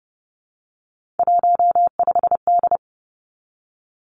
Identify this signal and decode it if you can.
Morse code, heard as a keyed tone, about 30 wpm, 700 Hz. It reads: 15D
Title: Three Challenging Letter-Number Combinations 30wpm